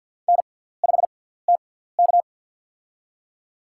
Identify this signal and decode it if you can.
Morse code, heard as a keyed tone, about 54 wpm, spaced as for 17 wpm, 700 Hz. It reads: N5TK
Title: USA Call Signs - 4 Characters - 6x Character Spacing - 50wpm